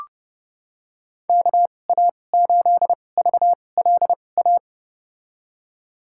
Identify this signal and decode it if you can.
Morse code, heard as a keyed tone, about 31 wpm, spaced as for 22 wpm, 700 Hz. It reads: KA8VLA
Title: USA Call Signs - 6 Characters - 2x Character Spacing - 30wpm